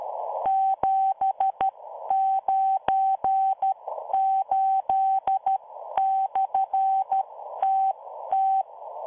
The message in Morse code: 798/TT